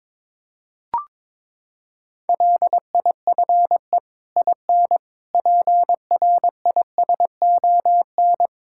Morse code, heard as a keyed tone, about 21 wpm, 700 Hz. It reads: LIFE IN PRISON
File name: practice / 22wpm / Sets of 8 Words - 22wpm